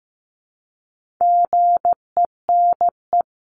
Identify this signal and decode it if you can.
Morse code, heard as a keyed tone, about 15 wpm, 700 Hz. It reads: GENE
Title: CWT Exchange 15wpm